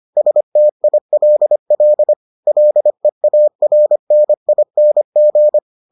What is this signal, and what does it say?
Morse code, heard as a keyed tone, about 25 wpm, 600 Hz. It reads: STILL LEARNING